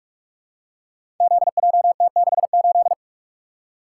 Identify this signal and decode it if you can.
Morse code, heard as a keyed tone, about 47 wpm, 700 Hz. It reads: ZJT68
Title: Code Groups - 5 Letter:Numbers - 45wpm